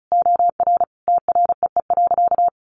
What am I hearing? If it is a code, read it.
Morse code, heard as a keyed tone, about 34 wpm, 700 Hz. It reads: OR TREE.
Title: A Sound Of Thunder 35wpm